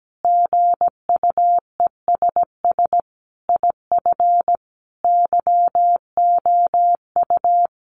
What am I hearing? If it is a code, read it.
Morse code, heard as a keyed tone, about 17 wpm, 700 Hz. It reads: GUESS IF YOU
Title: Sentences from Top 1500 Words 17wpm